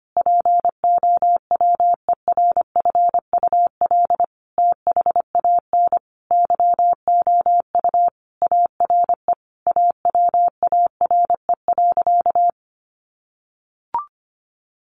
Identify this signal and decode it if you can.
Morse code, heard as a keyed tone, about 25 wpm, 700 Hz. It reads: POWERFUL THAN YOU ARE AWARE.
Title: Sentences from Top 1300 Words 25wpm